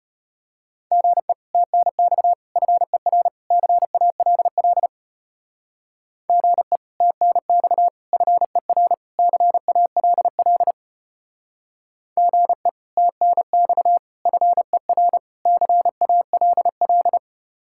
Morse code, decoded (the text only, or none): GE TNX FER CALL GE TNX FER CALL GE TNX FER CALL